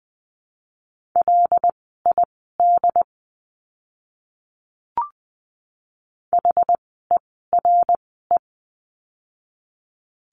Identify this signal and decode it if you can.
Morse code, heard as a keyed tone, about 20 wpm, spaced as for 14 wpm, 700 Hz. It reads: LID HERE
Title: 075 Lesson 13f - Letter L - Words 20wpm